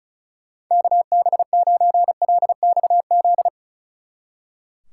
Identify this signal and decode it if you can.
Morse code, heard as a keyed tone, about 33 wpm, 700 Hz. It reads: KB9LXZ
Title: Speed Racing - USA Call Signs 35wpm